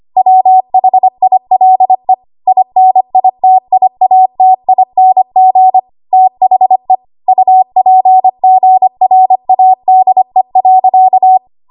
Morse code, heard as a keyed tone, about 25 wpm, 750 Hz. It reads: WHILE INITIATING THE UPGRADE.
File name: Archive / 25 WPM / 25WPM